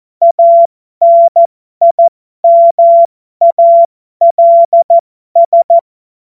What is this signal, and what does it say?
Morse code, heard as a keyed tone, about 14 wpm, spaced as for 12 wpm, 680 Hz. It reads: ANIMALS
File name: Morsekurs / Witze